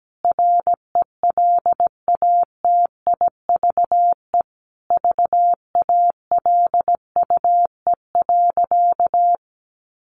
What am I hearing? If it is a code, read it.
Morse code, heard as a keyed tone, about 17 wpm, 700 Hz. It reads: RELATIVE VALUE.
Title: Sentences from Top 1400 Words 17wpm